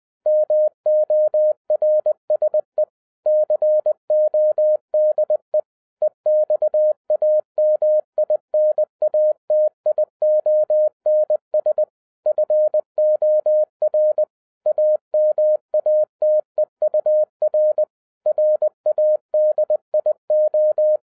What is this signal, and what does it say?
Morse code, heard as a keyed tone, about 20 wpm, 610 Hz. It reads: MORSE CODE EXAMINATIONS FOR AMATEUR RADIO